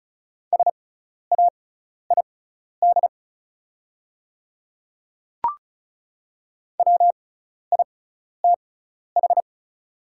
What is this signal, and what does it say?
Morse code, heard as a keyed tone, about 34 wpm, spaced as for 12 wpm, 700 Hz. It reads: SAID WITH
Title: 6x Character Spacing - Top 100 Words - 35wpm